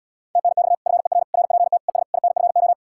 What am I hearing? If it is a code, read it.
Morse code, heard as a keyed tone, about 77 wpm, 700 Hz. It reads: TNX FER RPT ES INFO